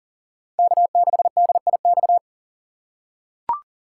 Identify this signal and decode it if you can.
Morse code, heard as a keyed tone, about 40 wpm, 700 Hz. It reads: K6DIX
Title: USA Call Signs - 3x Character Spacing - 40wpm